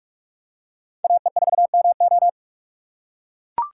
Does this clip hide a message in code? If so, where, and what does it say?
Yes: Morse code from 1.0 to 2.3 s, heard as a keyed tone, about 44 wpm, 700 Hz. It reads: AE4MO